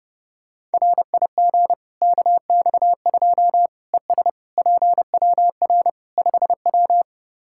Morse code, heard as a keyed tone, about 30 wpm, 700 Hz. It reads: RIG KX2 ES PWR 5W
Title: K1USN SST QSO Practice 30wpm